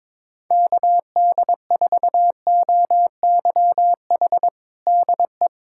Morse code, heard as a keyed tone, about 22 wpm, 700 Hz. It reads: KD4OYH DE
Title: USA Callsign DE Callsign - 6 Characters - 22wpm